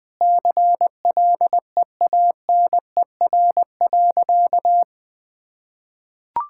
Morse code, heard as a keyed tone, about 20 wpm, 700 Hz. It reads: CLEANER.
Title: Pangram Challenge v2 - 20wpm